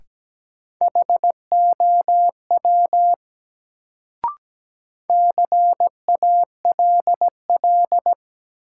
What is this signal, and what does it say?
Morse code, heard as a keyed tone, about 17 wpm, 700 Hz. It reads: HOW CALL
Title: Sentences from Top 100 Words - New Word Drill - 17wpm